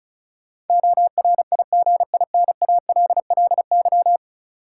Morse code, heard as a keyed tone, about 36 wpm, 700 Hz. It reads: ORIGINALLY
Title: Sentences from Top 1500 Words - New Word Drill Alt - 35wpm